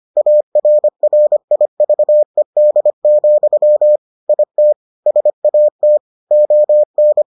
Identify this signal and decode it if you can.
Morse code, heard as a keyed tone, about 25 wpm, 600 Hz. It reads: ARRIVED, IT SAT ON